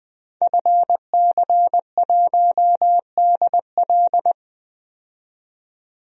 Morse code, heard as a keyed tone, about 20 wpm, 700 Hz. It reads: FC1DL
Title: DXCC-100 Call Signs - 5 Characters - 20wpm